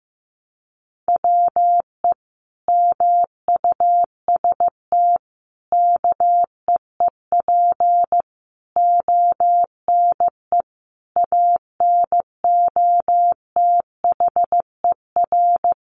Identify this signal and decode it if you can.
Morse code, heard as a keyed tone, about 15 wpm, 700 Hz. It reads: WE MUST KEEP ONE ANOTHER
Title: Sentences from Top 600 Words 15wpm